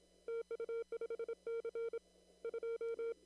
Morse code, heard as a keyed tone, about 26 wpm, 460 Hz. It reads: TU5C 2
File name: CW / kt5j wpx cw-op-k5pi